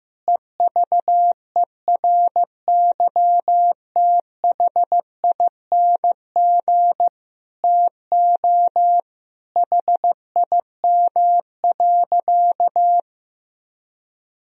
Morse code, heard as a keyed tone, about 14 wpm, 700 Hz. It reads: EVERYTHING TO HIM.
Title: Sentences from Top 1000 Words - Review Part 3 - 15wpm